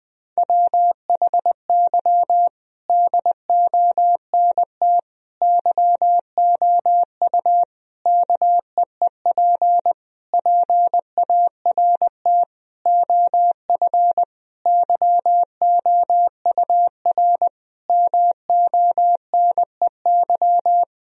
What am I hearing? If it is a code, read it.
Morse code, heard as a keyed tone, about 20 wpm, 700 Hz. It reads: WHY DONT YOU KEEP PART OF YOUR MONEY
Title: Sentences from Top 1000 Words - Review Part 4 - 20wpm